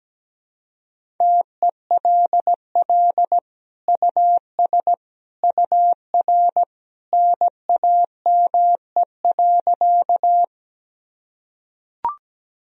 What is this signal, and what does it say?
Morse code, heard as a keyed tone, about 17 wpm, 700 Hz. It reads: TELL US UR NAME.
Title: Sentences from Top 200 Words and Abbreviations 17wpm